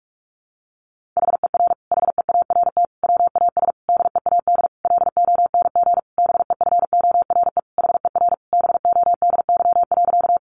Morse code, heard as a keyed tone, about 44 wpm, 700 Hz. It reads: HER HEART WAS DEAD LONG BEFORE HER BODY.
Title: Sentences from Top 1500 Words 45wpm